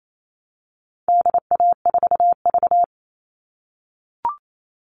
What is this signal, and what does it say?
Morse code, heard as a keyed tone, about 28 wpm, 700 Hz. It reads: DA4V
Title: All Cut Numbers - 4-Digit 28wpm